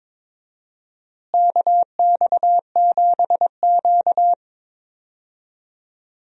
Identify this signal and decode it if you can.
Morse code, heard as a keyed tone, about 22 wpm, 700 Hz. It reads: KX7Q
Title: USA Call Signs - 4 Characters - Repeated 3x - 22wpm